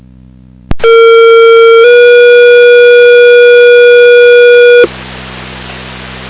Two-tone page, heard as a one-second tone, then a three-second tone, about 460 then 480 Hz, with noise. It is heard from 0.8 to 4.8 s.